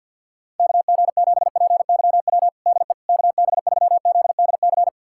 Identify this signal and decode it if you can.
Morse code, heard as a keyed tone, about 48 wpm, 700 Hz. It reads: KG7PYW DE KB3ZDC
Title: USA Callsign DE Callsign - 6 Characters - 50wpm